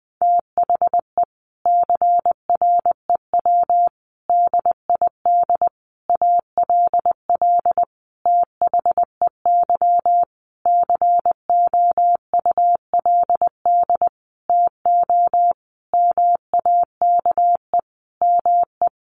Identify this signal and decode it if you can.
Morse code, heard as a keyed tone, about 20 wpm, 700 Hz. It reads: THE CREW DID ALL THEY COULD TO MAKE ME
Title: Sentences from Top 1500 Words 20wpm